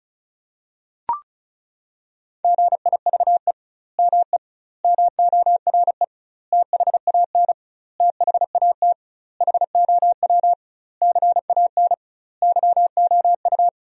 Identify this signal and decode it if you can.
Morse code, heard as a keyed tone, about 34 wpm, 700 Hz. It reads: GIVE ME MORE THAN THAT HOW CAN YOU
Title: 2x Word-Spacing - Sentences from Top 200 Words - 35wpm